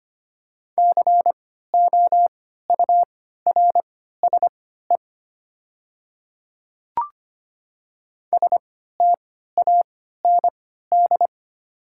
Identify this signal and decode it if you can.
Morse code, heard as a keyed tone, about 25 wpm, spaced as for 14 wpm, 690 Hz. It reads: COURSE STAND
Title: 3x Character Spacing - Top 500 Words - 25wpm